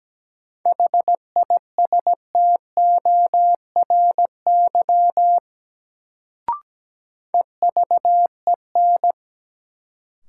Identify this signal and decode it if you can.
Morse code, heard as a keyed tone, about 17 wpm, 700 Hz. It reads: HISTORY EVEN
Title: Sentences from Top 600 Words - New Word Drill - 17wpm